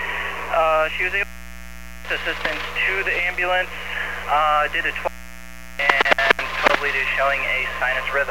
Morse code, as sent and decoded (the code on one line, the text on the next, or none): none